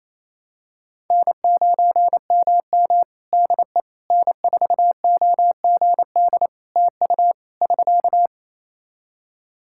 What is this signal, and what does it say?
Morse code, heard as a keyed tone, about 27 wpm, 700 Hz. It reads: N9MM DE N4OGD TU <SK>
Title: K1USN SST QSO Practice 28wpm